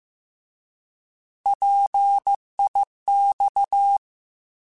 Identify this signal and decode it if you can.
Morse code, heard as a keyed tone, about 15 wpm, 800 Hz. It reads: PIX
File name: Alphabetical / Alphabetical mp3s / Ps @13